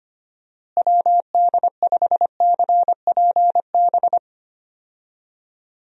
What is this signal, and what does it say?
Morse code, heard as a keyed tone, about 25 wpm, 700 Hz. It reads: WD5CPB
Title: USA Call Signs - 6 Characters - Repeated 2x - 25wpm